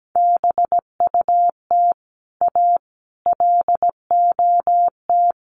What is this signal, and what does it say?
Morse code, heard as a keyed tone, about 17 wpm, 700 Hz. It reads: BUT A LOT
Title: Sets of 5 Words Encore2 17wpm